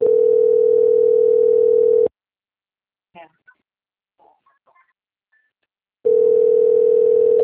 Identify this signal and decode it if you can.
Ringback tone (North America), 440 + 480 Hz, with speech.
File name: out- - - - - .30858